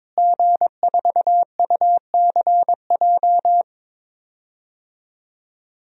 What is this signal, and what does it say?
Morse code, heard as a keyed tone, about 22 wpm, 700 Hz. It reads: G4UCJ
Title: DXCC-100 Call Signs - 5 Characters - Repeated 2x - 22wpm